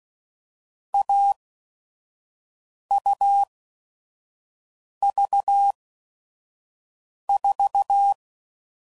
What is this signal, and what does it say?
Morse code, heard as a keyed tone, about 16 wpm, spaced as for 5 wpm, 800 Hz. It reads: AUV4
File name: Mixed / Short Runs / AUV AUV4